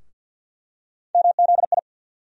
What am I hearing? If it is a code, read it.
Morse code, heard as a keyed tone, about 47 wpm, 700 Hz. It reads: MZI
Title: USA Call Sign Postfixes - 3 Letters 50wpm